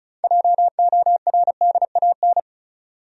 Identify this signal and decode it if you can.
Morse code, heard as a keyed tone, about 34 wpm, 700 Hz. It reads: JORDAN
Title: USA Common Names - 35wpm